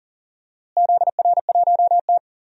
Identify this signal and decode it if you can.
Morse code, heard as a keyed tone, about 40 wpm, 700 Hz. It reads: ZR1T
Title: DXCC-100 Call Signs - 4 Characters - Repeated 3x - 40wpm